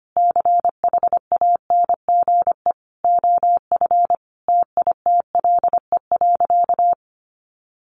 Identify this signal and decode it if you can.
Morse code, heard as a keyed tone, about 25 wpm, 700 Hz. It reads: CHANGE OF TITLE.
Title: Sentences from Top 1200 Words 25wpm